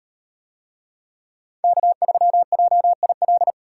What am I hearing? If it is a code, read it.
Morse code, heard as a keyed tone, about 38 wpm, 700 Hz. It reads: K3JIL